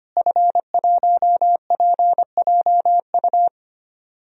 Morse code, heard as a keyed tone, about 25 wpm, 700 Hz. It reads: F1PJU
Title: DXCC-100 Call Signs - 5 Characters - Repeated 2x - 25wpm